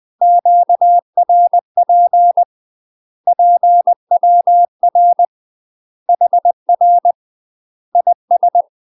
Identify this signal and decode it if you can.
Morse code, heard as a keyed tone, about 20 wpm, 700 Hz. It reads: QRP PWR HR IS
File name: UK MASTER QSOs TRAINING / 20WPM / UK MASTER QSOs 0067